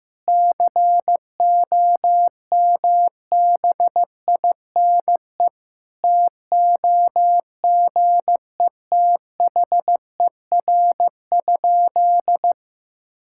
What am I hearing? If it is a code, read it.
Morse code, heard as a keyed tone, about 15 wpm, 700 Hz. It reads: COMBINE TOGETHER?
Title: Sentences from Top 1600 Words 15wpm